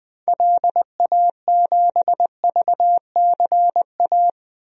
Morse code, heard as a keyed tone, about 20 wpm, 700 Hz. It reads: LA7VCA
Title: DXCC-100 Call Signs - 6 Characters - 20wpm